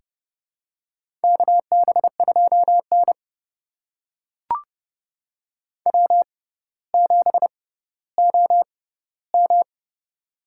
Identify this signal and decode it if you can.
Morse code, heard as a keyed tone, about 30 wpm, spaced as for 15 wpm, 700 Hz. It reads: KB2N W7OM
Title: USA Call Signs - 4 Characters - 6x Character Spacing - 30wpm